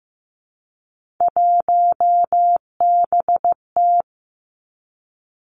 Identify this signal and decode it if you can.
Morse code, heard as a keyed tone, about 15 wpm, 700 Hz. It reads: 1BT
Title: Three Challenging Letter-Number Combinations 15wpm